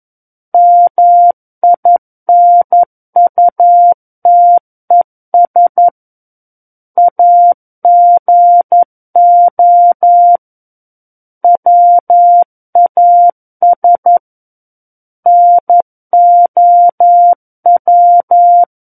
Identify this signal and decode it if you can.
Morse code, heard as a keyed tone, about 11 wpm, 700 Hz. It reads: MINUTES AGO WAS NOW
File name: 11 WPM